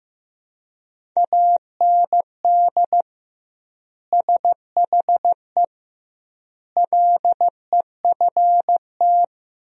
Morse code, heard as a keyed tone, about 14 wpm, 700 Hz. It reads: AND SHE LEFT
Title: 2x Word-Spacing - Sentences from Top 300 Words - 15wpm